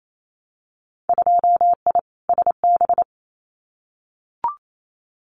Morse code, heard as a keyed tone, about 28 wpm, 700 Hz. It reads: 2I SB
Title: Winter Field Day Exchanges Encore 28wpm